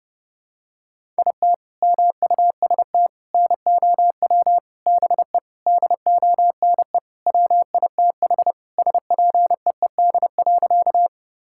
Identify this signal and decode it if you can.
Morse code, heard as a keyed tone, about 30 wpm, 700 Hz. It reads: IT MUST NOW BE DONE WITH SPEED.
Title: Sentences from Top 1000 Words - Review Part 5 - 30wpm